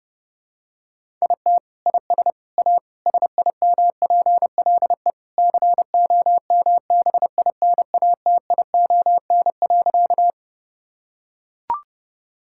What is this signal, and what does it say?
Morse code, heard as a keyed tone, about 30 wpm, 700 Hz. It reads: IT IS A SIMPLE COMBINATION.
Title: Sentences from Top 1000 Words - Review Part 6 - 30wpm